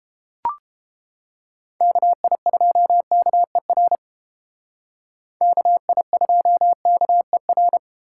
Morse code, heard as a keyed tone, about 33 wpm, 700 Hz. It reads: KI2KER KI2KER